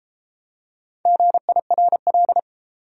Morse code, heard as a keyed tone, about 31 wpm, 700 Hz. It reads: GIRL